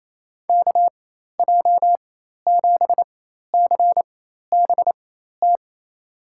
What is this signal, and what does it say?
Morse code, heard as a keyed tone, about 28 wpm, spaced as for 13 wpm, 700 Hz. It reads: KJ7CBT
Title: USA Call Signs - 4x Character Spacing - 28wpm